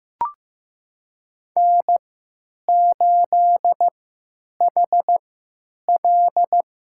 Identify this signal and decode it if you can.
Morse code, heard as a keyed tone, about 15 wpm, spaced as for 9 wpm, 700 Hz. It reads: N8HL
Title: USA Call Signs - 3x Character Spacing - 15wpm